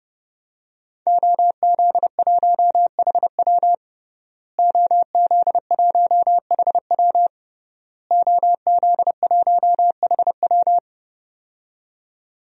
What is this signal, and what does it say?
Morse code, heard as a keyed tone, about 32 wpm, 700 Hz. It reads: OZ1HW OZ1HW OZ1HW
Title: DXCC-100 Call Signs - 5 Characters - Repeated 3x - 30wpm